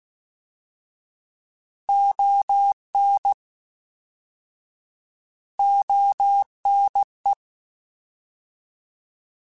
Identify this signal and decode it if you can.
Morse code, heard as a keyed tone, about 16 wpm, 800 Hz. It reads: ON ONE